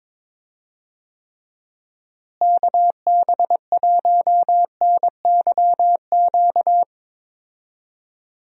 Morse code, heard as a keyed tone, about 22 wpm, 700 Hz. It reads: KB1NYQ
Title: USA Call Signs - 6 Characters - Repeated 2x - 22wpm